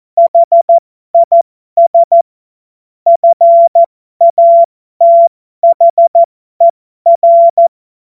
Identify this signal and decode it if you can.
Morse code, heard as a keyed tone, about 14 wpm, spaced as for 12 wpm, 680 Hz. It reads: HIS FATHER
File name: Morsekurs / Witze